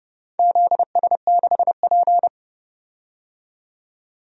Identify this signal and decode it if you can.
Morse code, heard as a keyed tone, about 30 wpm, 700 Hz. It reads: ZS6P
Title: DXCC-100 Call Signs - 4 Characters - 30wpm